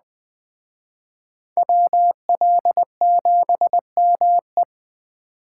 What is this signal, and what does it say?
Morse code, heard as a keyed tone, about 20 wpm, 700 Hz. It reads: WL7ME
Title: Speed Racing - USA 2x2 Call Signs 17wpm